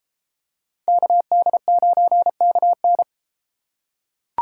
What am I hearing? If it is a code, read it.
Morse code, heard as a keyed tone, about 32 wpm, 700 Hz. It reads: KD9KN